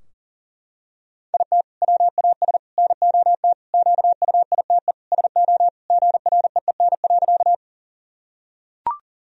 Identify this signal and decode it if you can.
Morse code, heard as a keyed tone, about 37 wpm, 700 Hz. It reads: IT WAS NOT QUITE SO GREEN.
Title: Sentences from Top 1000 Words - Review Part 2 - 40wpm